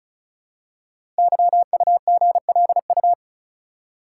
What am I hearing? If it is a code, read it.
Morse code, heard as a keyed tone, about 36 wpm, 690 Hz. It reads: YUGLU
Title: Code Groups - 5 Letter:Numbers - 35wpm